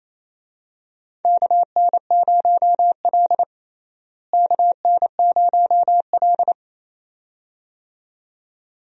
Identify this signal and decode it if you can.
Morse code, heard as a keyed tone, about 28 wpm, 700 Hz. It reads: KN0L KN0L
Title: USA Call Signs - 4 Characters - Repeated 2x - 28wpm